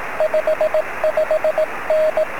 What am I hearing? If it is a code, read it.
Morse code, heard as a keyed tone, about 18 wpm, 620 Hz, with radio noise. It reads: 55N
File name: sm7lcb / 02-sm0fzh 23cm